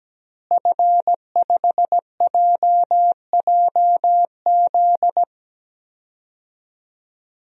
Morse code, heard as a keyed tone, about 17 wpm, 700 Hz. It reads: F5JJZ
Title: DXCC-100 Call Signs - 5 Characters - Repeated 3x - 17wpm